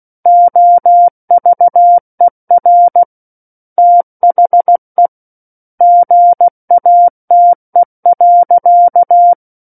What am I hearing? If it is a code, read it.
Morse code, heard as a keyed tone, about 16 wpm, 700 Hz. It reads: OVER THE GATE.
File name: Book One / 07-Chapter 7 - 16 WPM